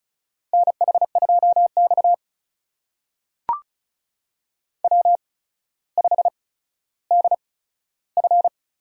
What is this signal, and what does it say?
Morse code, heard as a keyed tone, about 36 wpm, spaced as for 15 wpm, 700 Hz. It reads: NH2X W5DF
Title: USA Call Signs - 4 Characters - 8x Character Spacing - 35wpm